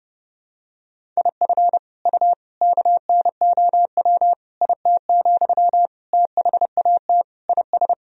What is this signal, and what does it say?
Morse code, heard as a keyed tone, about 30 wpm, 700 Hz. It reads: IF U KNOW IT, THAT IS
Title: Sentences from Top 400 Words and Abbreviations 30wpm